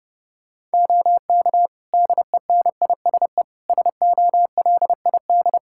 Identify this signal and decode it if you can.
Morse code, heard as a keyed tone, about 30 wpm, 700 Hz. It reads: OK DENISE SOLID
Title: Practice QSO - Exchange 3 - More Info - 30wpm